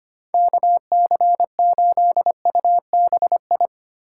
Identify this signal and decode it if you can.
Morse code, heard as a keyed tone, about 25 wpm, 700 Hz. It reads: KC8UBI
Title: USA Call Signs - 6 Characters - 3x Character Spacing - 25wpm